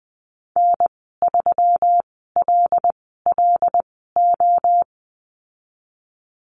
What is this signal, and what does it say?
Morse code, heard as a keyed tone, about 20 wpm, spaced as for 14 wpm, 700 Hz. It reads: N3LLO
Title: 219 Lesson 35g - Number 0 - Call Signs 20wpm